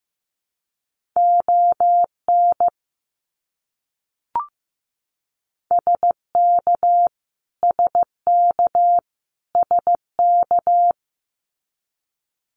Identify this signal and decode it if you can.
Morse code, heard as a keyed tone, about 15 wpm, 700 Hz. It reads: ON SK SK SK
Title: Canadian Provinces and Territories - Repeated 3x - 15wpm